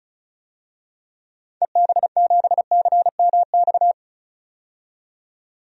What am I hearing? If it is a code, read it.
Morse code, heard as a keyed tone, about 36 wpm, 700 Hz. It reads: EB7CMX
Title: DXCC-100 Call Signs - 6 Characters - 35wpm